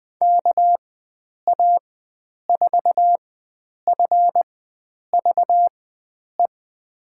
Morse code, heard as a keyed tone, about 20 wpm, spaced as for 9 wpm, 700 Hz. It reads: KA4FVE
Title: USA Call Signs - 6 Characters - 4x Character Spacing - 20wpm